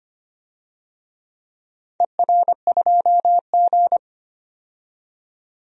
Morse code, heard as a keyed tone, about 25 wpm, 700 Hz. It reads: ER2G
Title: DXCC-100 Call Signs - 4 Characters - 25wpm